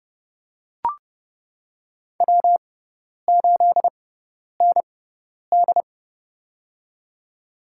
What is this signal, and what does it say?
Morse code, heard as a keyed tone, about 32 wpm, spaced as for 10 wpm, 700 Hz. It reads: W8ND
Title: USA Call Signs - 4 Characters - 6x Character Spacing - 30wpm